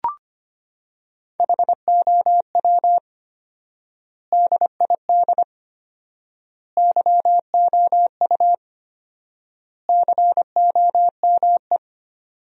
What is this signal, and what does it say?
Morse code, heard as a keyed tone, about 25 wpm, 700 Hz. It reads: HOW DID YOU COME
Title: 4x Word-Spacing - Sentences from Top 100 Words - 25wpm